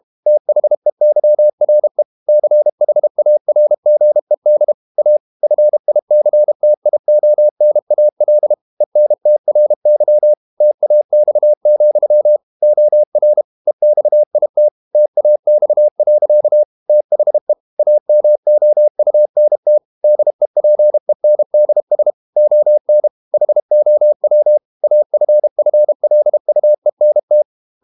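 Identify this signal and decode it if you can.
Morse code, heard as a keyed tone, about 32 wpm, 600 Hz. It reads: THEYRE CHARGED A FICTIONAL ENTRY TAX, OR EXIT TAX. THE AMOUNT DEPENDS ON HOW AFFLUENT
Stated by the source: Morse code practice at 32 wpm from CW med Gnister, reading Ken Follett's novel Never in English.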